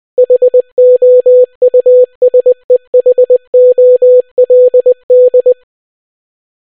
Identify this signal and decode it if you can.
Morse code, heard as a keyed tone, about 20 wpm, 500 Hz. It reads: HOUSEHOLD